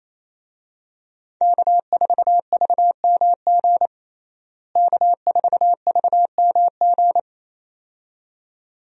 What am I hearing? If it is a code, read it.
Morse code, heard as a keyed tone, about 28 wpm, 700 Hz. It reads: K4VMG K4VMG
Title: USA Call Signs - 5 Characters - Repeated 2x - 28wpm